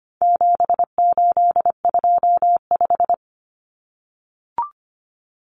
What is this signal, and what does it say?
Morse code, heard as a keyed tone, about 25 wpm, 700 Hz. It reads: 7825
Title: Numbers - 4-Digit 25wpm